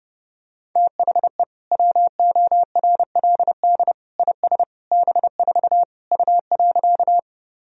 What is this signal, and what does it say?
Morse code, heard as a keyed tone, about 30 wpm, 700 Hz. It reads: THE WORLD IS B4 U.
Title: Sentences from Top 200 Words and Abbreviations 30wpm